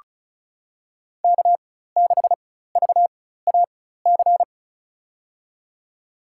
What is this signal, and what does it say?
Morse code, heard as a keyed tone, about 36 wpm, spaced as for 16 wpm, 690 Hz. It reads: K6VAC
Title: USA Call Signs - 4x Character Spacing - 35wpm